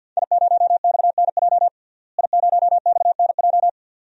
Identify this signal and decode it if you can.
Morse code, heard as a keyed tone, about 46 wpm, 700 Hz. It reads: I0XNJ I0XNJ
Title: DXCC-100 Call Signs - 5 Characters - Repeated 2x - 50wpm